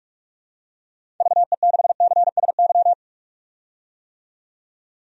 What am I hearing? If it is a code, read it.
Morse code, heard as a keyed tone, about 46 wpm, 700 Hz. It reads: VE6CSY
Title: DXCC-100 Call Signs - 6 Characters - 45wpm